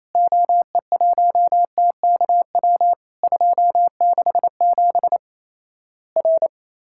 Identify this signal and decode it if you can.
Morse code, heard as a keyed tone, about 28 wpm, 700 Hz. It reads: OE1TKW 267 R
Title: Worked All Europe DX Contest - QTC Exchange - 28wpm